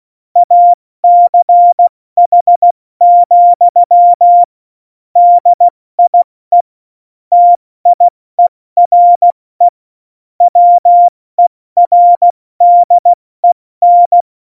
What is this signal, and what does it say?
Morse code, heard as a keyed tone, about 16 wpm, 700 Hz. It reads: ACH, DIE TIERE WERDEN
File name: Morsekurs / Witze / Witz 21. Januar 2020, 70 F80